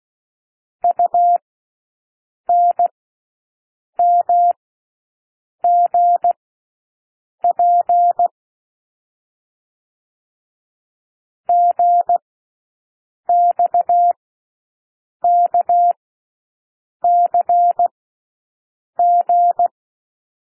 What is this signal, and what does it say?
Morse code, heard as a keyed tone, about 16 wpm, spaced as for 6 wpm, 700 Hz. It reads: UNMGP GXKCG
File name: cwmp3 / L1 SOUND LETTERS 16 wpm